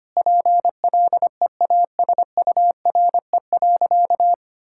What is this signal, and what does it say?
Morse code, heard as a keyed tone, about 25 wpm, 700 Hz. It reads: PLEASURE.
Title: Sentences from Top 1600 Words 25wpm